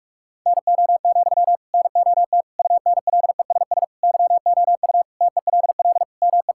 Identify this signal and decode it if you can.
Morse code, heard as a keyed tone, about 46 wpm, 700 Hz. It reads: NO, NOT UNLESS YOU TELL ME
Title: Sentences from Top 1000 Words - Review Part 6 - 45wpm